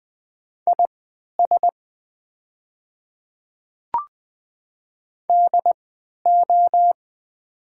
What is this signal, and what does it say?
Morse code, heard as a keyed tone, about 21 wpm, spaced as for 11 wpm, 700 Hz. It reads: IS DO
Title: 3x Character Spacing - Top 200 Words - 20wpm